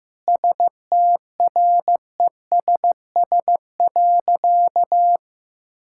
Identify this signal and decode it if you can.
Morse code, heard as a keyed tone, about 15 wpm, 700 Hz. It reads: STRESS.
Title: Sentences from Top 1000 Words - Review Part 6 - 15wpm